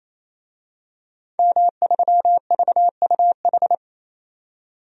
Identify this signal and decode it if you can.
Morse code, heard as a keyed tone, about 28 wpm, 700 Hz. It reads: M3VUH
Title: DXCC-100 Call Signs - 5 Characters - 28wpm